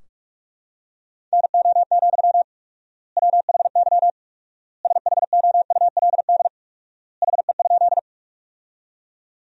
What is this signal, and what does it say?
Morse code, heard as a keyed tone, about 46 wpm, 700 Hz. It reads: NO, WHY SHOULD HE?
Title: 4x Word-Spacing - Sentences from Top 300 Words - 45wpm